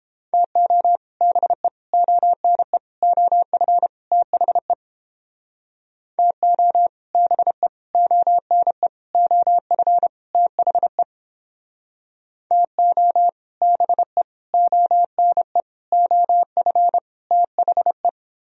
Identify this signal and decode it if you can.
Morse code, heard as a keyed tone, about 30 wpm, 700 Hz. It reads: TO BE ONE OF THE TO BE ONE OF THE TO BE ONE OF THE